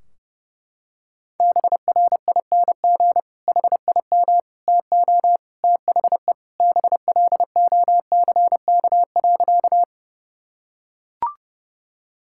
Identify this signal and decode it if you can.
Morse code, heard as a keyed tone, about 30 wpm, 700 Hz. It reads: BRING HIM TO THE BLOCK.
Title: Sentences from Top 1600 Words 30wpm